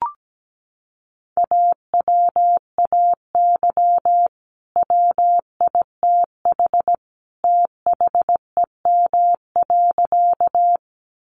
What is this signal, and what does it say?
Morse code, heard as a keyed tone, about 17 wpm, 700 Hz. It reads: AWAY WITH THEM.
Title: Sentences from Top 600 Words 17wpm